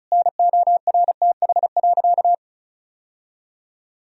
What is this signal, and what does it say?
Morse code, heard as a keyed tone, about 36 wpm, 700 Hz. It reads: NORTH.
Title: Sentences from Top 1000 Words - Review Part 1 - 35wpm